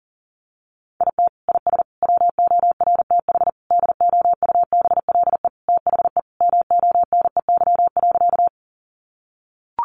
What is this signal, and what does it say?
Morse code, heard as a keyed tone, about 40 wpm, 700 Hz. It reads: IT IS WORTH DOUBLE THE MONEY.
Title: Sentences from Top 1400 Words 40wpm